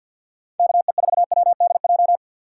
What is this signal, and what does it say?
Morse code, heard as a keyed tone, about 65 wpm, spaced as for 57 wpm, 700 Hz. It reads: KE4WDJ
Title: Speed Racing - USA Call Signs 50wpm